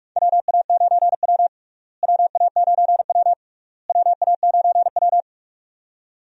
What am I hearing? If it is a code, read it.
Morse code, heard as a keyed tone, about 42 wpm, 700 Hz. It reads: WA9W WA9W WA9W
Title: USA Call Signs - 4 Characters - Repeated 3x - 45wpm